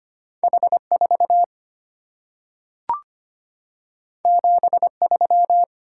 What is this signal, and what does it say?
Morse code, heard as a keyed tone, about 25 wpm, 690 Hz. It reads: H4 73
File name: Two Letter-Number Combinations - 25wpm